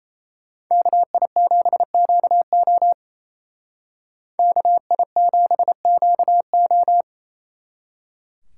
Morse code, heard as a keyed tone, about 36 wpm, 700 Hz. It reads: KI7QO KI7QO